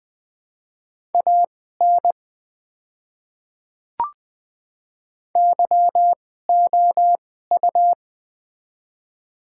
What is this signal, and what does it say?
Morse code, heard as a keyed tone, about 20 wpm, spaced as for 14 wpm, 690 Hz. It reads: AN YOU
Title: 2x Character Spacing - Top 100 Words - 20wpm